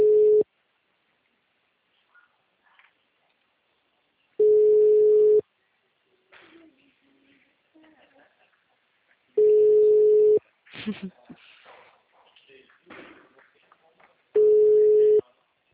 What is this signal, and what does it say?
Ringback tone (Europe, 425 Hz), with speech.